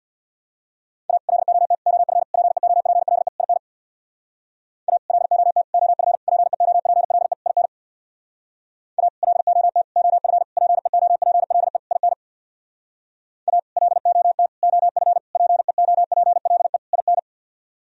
Morse code, heard as a keyed tone, about 62 wpm, 700 Hz. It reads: A LOT OF PEOPLE IN A LOT OF PEOPLE IN A LOT OF PEOPLE IN A LOT OF PEOPLE IN
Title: Speed-Racing - Sets of 5 Words 50wpm